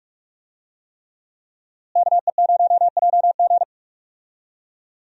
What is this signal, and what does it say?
Morse code, heard as a keyed tone, about 43 wpm, 700 Hz. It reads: KE0JG
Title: USA Call Signs - 5 Characters - Repeated 3x - 45wpm